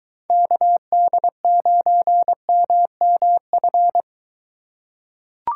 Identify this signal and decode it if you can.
Morse code, heard as a keyed tone, about 22 wpm, 700 Hz. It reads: KD9MMF